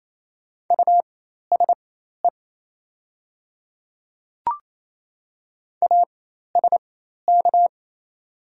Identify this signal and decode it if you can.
Morse code, heard as a keyed tone, about 28 wpm, spaced as for 13 wpm, 690 Hz. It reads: USE ASK
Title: 4x Character Spacing - Top 200 Words - 28wpm